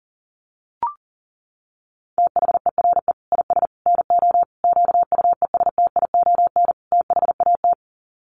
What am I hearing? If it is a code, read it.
Morse code, heard as a keyed tone, about 40 wpm, 700 Hz. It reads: THERE IS NO QUESTION THAT
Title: Sets of 5 Words Encore 40wpm